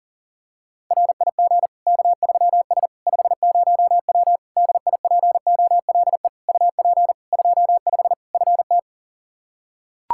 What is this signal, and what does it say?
Morse code, heard as a keyed tone, about 40 wpm, 700 Hz. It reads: RIG K3S 50W DIPOLE UP 25 FT
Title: Practice QSO - Exchange 2 - Info - 40wpm